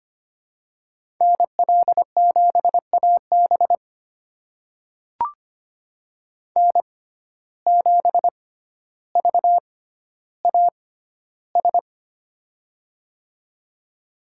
Morse code, heard as a keyed tone, about 25 wpm, spaced as for 13 wpm, 700 Hz. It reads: NL7AB N7VAS
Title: USA Call Signs - 5 Characters - 6x Character Spacing - 25wpm